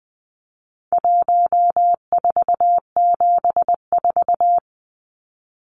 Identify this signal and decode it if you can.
Morse code, heard as a keyed tone, about 20 wpm, 700 Hz. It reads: 1474
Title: Numbers - 4-Digit 20wpm